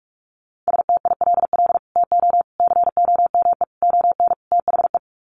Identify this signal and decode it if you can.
Morse code, heard as a keyed tone, about 44 wpm, 700 Hz. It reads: STILL TO COME ON THE
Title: Sets of 5 Words Encore2 45wpm